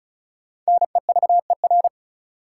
Morse code, heard as a keyed tone, about 35 wpm, 700 Hz. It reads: NEVER
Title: Sentences from Top 300 Words - New Word Drill - 35wpm